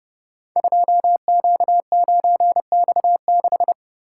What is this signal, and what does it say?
Morse code, heard as a keyed tone, about 30 wpm, 700 Hz. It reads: 2Q9X6
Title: Code Groups - 5 Letter:Numbers - 30wpm